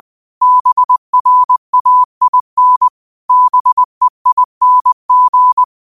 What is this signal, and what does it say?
Morse code, heard as a keyed tone, about 20 wpm, 1000 Hz. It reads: BRAIN BEING
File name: qotdfiles / 20wpm